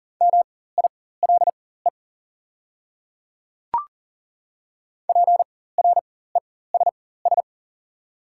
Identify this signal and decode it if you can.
Morse code, heard as a keyed tone, about 40 wpm, spaced as for 19 wpm, 700 Hz. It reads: MILE PRESS
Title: 4x Character Spacing - Top 300 Words - 40wpm